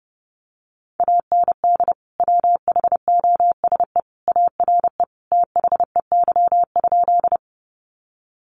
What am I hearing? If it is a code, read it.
Morse code, heard as a keyed tone, about 30 wpm, 700 Hz. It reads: AND WHOSE ARE THEY?
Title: Sentences from Top 1200 Words 30wpm